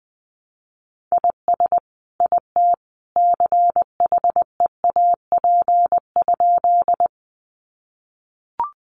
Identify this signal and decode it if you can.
Morse code, heard as a keyed tone, about 20 wpm, 700 Hz. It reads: IS IT CHEAP?
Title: Sentences from Top 1200 Words 20wpm